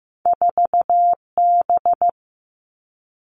Morse code, heard as a keyed tone, about 15 wpm, 700 Hz. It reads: 4B
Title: Two Challenging Letter-Number Combinations 15wpm